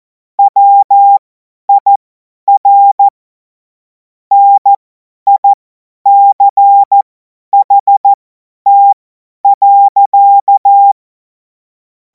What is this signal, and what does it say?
Morse code, heard as a keyed tone, about 14 wpm, spaced as for 10 wpm, 800 Hz. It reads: WIR NICHT.
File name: Morsekurs / Witze / Witz 14. April 2026 50 F70